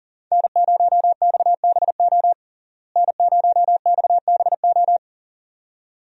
Morse code, heard as a keyed tone, about 37 wpm, 700 Hz. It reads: N0XBO N0XBO
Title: USA Call Signs - 5 Characters - Repeated 2x - 40wpm